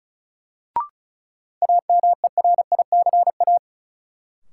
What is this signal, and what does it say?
Morse code, heard as a keyed tone, about 36 wpm, 700 Hz. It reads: AMERICA
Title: Sets of 5 Words - New Word Drill - 35wpm